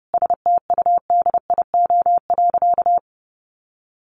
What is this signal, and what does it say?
Morse code, heard as a keyed tone, about 30 wpm, 700 Hz. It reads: STUDIO.
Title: Sentences from Top 1200 Words 30wpm